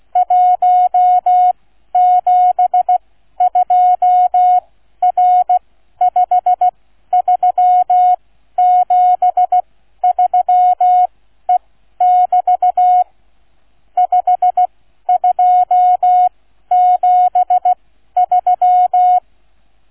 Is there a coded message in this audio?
Yes: Morse code from 0.2 to 19.2 s, heard as a keyed tone, about 16 wpm, spaced as for 12 wpm, 730 Hz. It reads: 172R5373E<BT> 5273